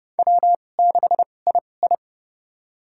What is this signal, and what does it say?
Morse code, heard as a keyed tone, about 30 wpm, spaced as for 22 wpm, 700 Hz. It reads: W6II
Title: USA Call Signs - 4 Characters - 2x Character Spacing - 30wpm